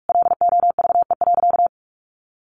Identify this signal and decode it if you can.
Morse code, heard as a keyed tone, about 44 wpm, 700 Hz. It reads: LOVE.
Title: Sentences from Top 1400 Words 45wpm